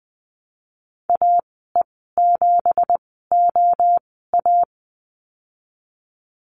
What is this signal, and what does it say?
Morse code, heard as a keyed tone, about 20 wpm, spaced as for 15 wpm, 700 Hz. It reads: AE7OA
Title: 179 Lesson 29g - Slash - Call Signs 20wpm